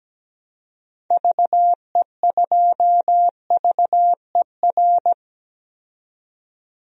Morse code, heard as a keyed tone, about 17 wpm, 700 Hz. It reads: VE2VER
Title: DXCC-100 Call Signs - 6 Characters - 17wpm